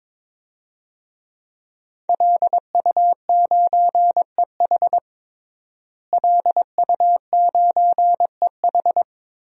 Morse code, heard as a keyed tone, about 22 wpm, 700 Hz. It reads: LU9EH LU9EH
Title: DXCC-100 Call Signs - 5 Characters - Repeated 3x - 22wpm